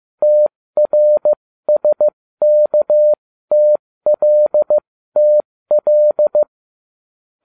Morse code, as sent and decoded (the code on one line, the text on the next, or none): - .-. ... -.- - .-.. - .-..
TRSKTLTL